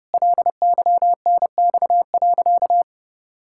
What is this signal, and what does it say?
Morse code, heard as a keyed tone, about 30 wpm, 700 Hz. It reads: LYNX.
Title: Pangram Challenge v2 - 30wpm